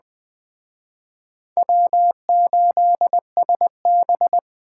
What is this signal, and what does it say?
Morse code, heard as a keyed tone, about 21 wpm, 700 Hz. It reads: W8SB
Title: Speed Racing - USA 1x2 Call Signs 17wpm